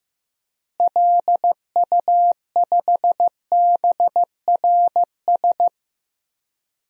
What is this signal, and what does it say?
Morse code, heard as a keyed tone, about 15 wpm, 700 Hz. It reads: LU5BRS
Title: DXCC-100 Call Signs - 6 Characters - Repeated 3x - 15wpm